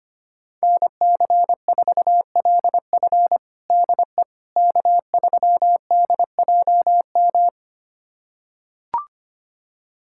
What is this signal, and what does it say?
Morse code, heard as a keyed tone, about 25 wpm, 700 Hz. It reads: NC4LF DE K3DJM
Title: USA Callsign DE Callsign - 5 Characters - 25wpm